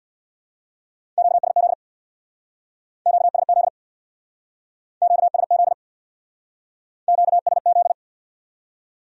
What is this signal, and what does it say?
Morse code, heard as a keyed tone, about 67 wpm, 700 Hz. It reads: QSZ QSZ QSZ QSZ